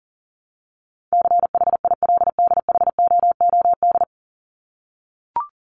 Morse code, heard as a keyed tone, about 40 wpm, 700 Hz. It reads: CHILDHOOD
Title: Top 1751-2000 Words 40wpm